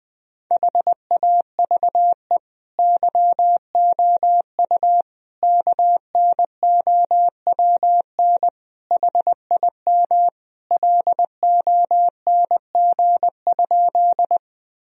Morse code, heard as a keyed tone, about 21 wpm, 700 Hz. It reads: HAVE YOU KNOWN HIM LONG?
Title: Sentences from Top 1000 Words - Review Part 3 - 20wpm